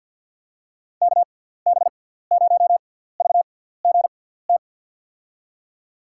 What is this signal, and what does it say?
Morse code, heard as a keyed tone, about 47 wpm, spaced as for 17 wpm, 700 Hz. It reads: KB0VGT
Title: USA Call Signs - 6 Characters - 6x Character Spacing - 50wpm